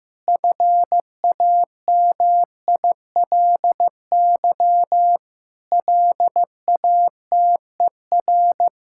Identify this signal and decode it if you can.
Morse code, heard as a keyed tone, about 15 wpm, 700 Hz. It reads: FAMILY LATER
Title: Sentences from Top 1600 Words 15wpm